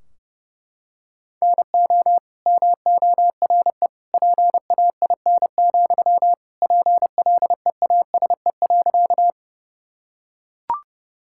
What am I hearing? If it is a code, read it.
Morse code, heard as a keyed tone, about 32 wpm, 700 Hz. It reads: NO MORE PAIN, PLEASE.
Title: Sentences from Top 1600 Words 30wpm